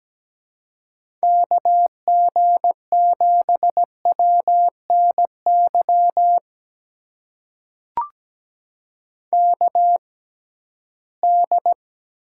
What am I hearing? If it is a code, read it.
Morse code, heard as a keyed tone, about 17 wpm, spaced as for 13 wpm, 700 Hz. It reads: KG7WNY KD
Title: USA Call Signs - 6x Character Spacing - 17wpm